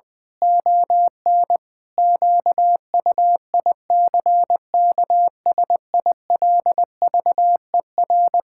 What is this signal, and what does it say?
Morse code, heard as a keyed tone, about 20 wpm, 700 Hz. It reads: ON QUICKSILVER
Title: Pangram Challenge v2 - 20wpm